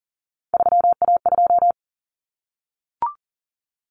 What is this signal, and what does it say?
Morse code, heard as a keyed tone, about 40 wpm, 700 Hz. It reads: 3A2
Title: Common Cut Numbers - 3-Digit 40wpm